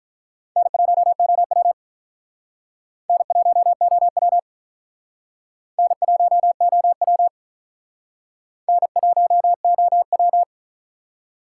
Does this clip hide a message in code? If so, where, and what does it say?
Yes: Morse code from 0.6 to 10.4 s, heard as a keyed tone, about 45 wpm, 700 Hz. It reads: N1OW N1OW N1OW N1OW